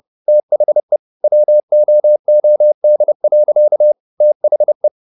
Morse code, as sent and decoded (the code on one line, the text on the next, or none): - .... / .-- --- --- -.. .-.-.- / - .... .
THE WOOD. THE